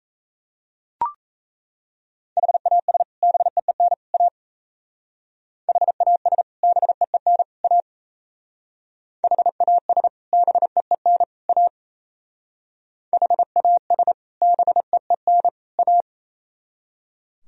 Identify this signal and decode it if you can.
Morse code, heard as a keyed tone, about 33 wpm, 700 Hz. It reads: HAS BEEN A HAS BEEN A HAS BEEN A HAS BEEN A
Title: Speed-Racing - Sets of 3 Words 28wpm